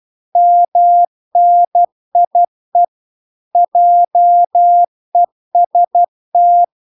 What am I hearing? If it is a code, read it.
Morse code, heard as a keyed tone, about 12 wpm, 700 Hz. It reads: MNIE JEST